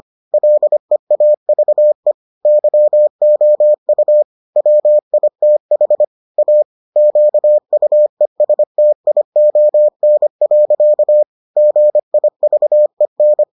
Morse code, heard as a keyed tone, about 25 wpm, 600 Hz. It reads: LEAVE YOU WITH A QUESTION. GIVEN